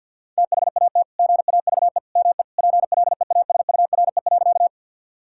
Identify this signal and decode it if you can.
Morse code, heard as a keyed tone, about 50 wpm, 700 Hz. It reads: THAT GAVE ME PLEASURE.
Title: Sentences from Top 1600 Words 50wpm